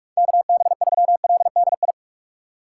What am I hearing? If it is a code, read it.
Morse code, heard as a keyed tone, about 46 wpm, 700 Hz. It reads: KB3LDI
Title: USA Call Signs - 2x Character Spacing - 45wpm